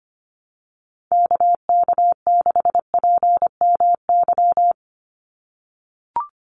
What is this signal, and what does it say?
Morse code, heard as a keyed tone, about 25 wpm, 700 Hz. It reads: KK6PMY
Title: USA Call Signs 25wpm